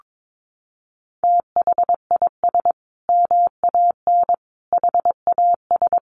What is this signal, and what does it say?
Morse code, heard as a keyed tone, about 22 wpm, 700 Hz. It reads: THIS MAN HAS
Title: Sentences from Top 1300 Words 22wpm